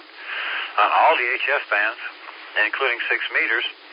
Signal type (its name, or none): Emergency Alert System (SAME) tones